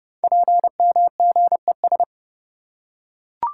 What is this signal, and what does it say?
Morse code, heard as a keyed tone, about 30 wpm, 700 Hz. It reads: PMGES
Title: Code Groups - 5 Letters - 30wpm